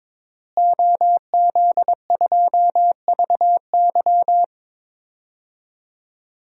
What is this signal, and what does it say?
Morse code, heard as a keyed tone, about 21 wpm, 700 Hz. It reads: OZ2VY
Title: DXCC-100 Call Signs - 5 Characters - Repeated 3x - 22wpm